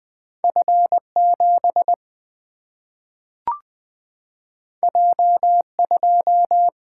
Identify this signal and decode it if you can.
Morse code, heard as a keyed tone, about 20 wpm, 700 Hz. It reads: F7 J2
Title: Single-Letter Single-Number 20wpm